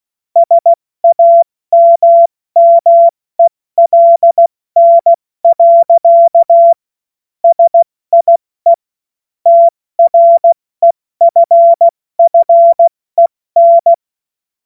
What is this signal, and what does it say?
Morse code, heard as a keyed tone, about 16 wpm, 680 Hz. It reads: SAMMELN. SIE TREFFEN
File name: Witz-23-September-2025-70-F80